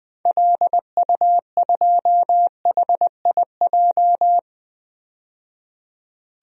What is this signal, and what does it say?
Morse code, heard as a keyed tone, about 20 wpm, 700 Hz. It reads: LU2HIJ
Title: DXCC-100 Call Signs - 6 Characters - 20wpm